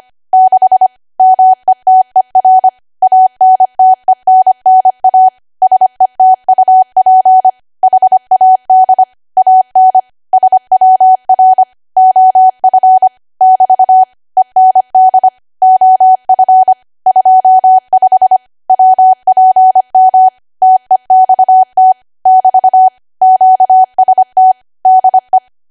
Morse code, heard as a keyed tone, about 25 wpm, 750 Hz. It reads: 6 METER ANTENNA SETUP HAD AN SWR OF <BT> END OF 25 WPM TEXT <BT> QST DE
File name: Archive / 25 WPM / 25WPM